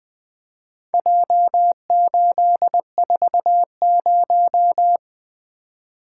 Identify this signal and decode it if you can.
Morse code, heard as a keyed tone, about 20 wpm, 700 Hz. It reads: J840
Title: Four Letter-Number Combinations - 20wpm